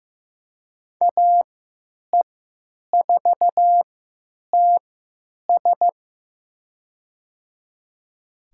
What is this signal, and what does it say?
Morse code, heard as a keyed tone, about 15 wpm, spaced as for 9 wpm, 700 Hz. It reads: AE4TS
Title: USA Call Signs - 5 Characters - 3x Character Spacing - 15wpm